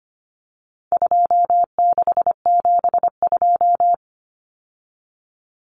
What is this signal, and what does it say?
Morse code, heard as a keyed tone, about 25 wpm, 700 Hz. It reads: 2672
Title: Numbers - 4-Digit 25wpm